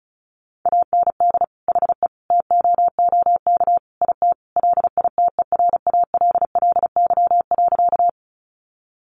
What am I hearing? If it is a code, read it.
Morse code, heard as a keyed tone, about 34 wpm, 700 Hz. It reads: AND HE TOOK IT LITERALLY.
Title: Sentences from Top 1400 Words 35wpm